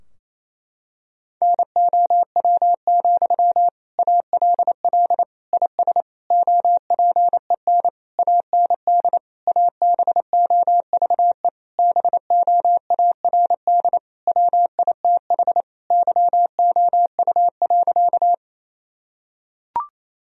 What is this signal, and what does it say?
Morse code, heard as a keyed tone, about 28 wpm, 700 Hz. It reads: NOW, ALL IS OPEN AND ABOVE BOARD WITH YOU.
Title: Sentences from Top 1000 Words - Review Part 4 - 28wpm